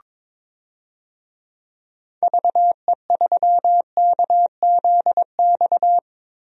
Morse code, heard as a keyed tone, about 22 wpm, 700 Hz. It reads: VE3KZX
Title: DXCC-100 Call Signs - 6 Characters - Repeated 3x - 22wpm